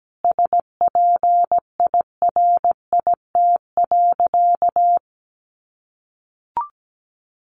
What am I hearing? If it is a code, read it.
Morse code, heard as a keyed tone, about 17 wpm, 700 Hz. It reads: SPIRIT.
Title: Sentences from Top 1400 Words 17wpm